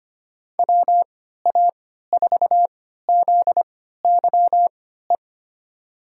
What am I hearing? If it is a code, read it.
Morse code, heard as a keyed tone, about 25 wpm, spaced as for 14 wpm, 700 Hz. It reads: WA4ZYE
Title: USA Call Signs - 6 Characters - 3x Character Spacing - 25wpm